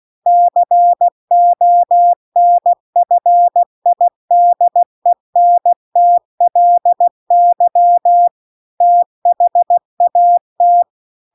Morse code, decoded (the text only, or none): CONFIDENTLY THAT